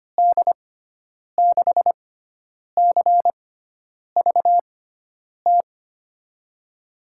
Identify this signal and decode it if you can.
Morse code, heard as a keyed tone, about 25 wpm, spaced as for 9 wpm, 700 Hz. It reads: D6CVT
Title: USA Call Signs - 6 Characters - 6x Character Spacing - 25wpm